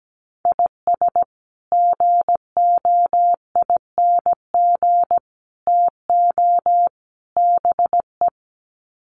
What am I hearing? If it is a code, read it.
Morse code, heard as a keyed tone, about 17 wpm, 700 Hz. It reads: IS GOING TO BE
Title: Sets of 4 Words 17wpm